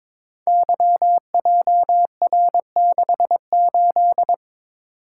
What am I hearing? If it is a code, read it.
Morse code, heard as a keyed tone, about 22 wpm, 700 Hz. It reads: YJR68
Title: Code Groups - 5 Letter:Numbers - 22wpm